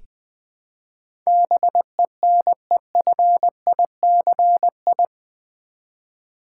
Morse code, heard as a keyed tone, about 20 wpm, 700 Hz. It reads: BENEFICI
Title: Build A Word - 4 Letters At A Time - 11-Letter Words - 20wpm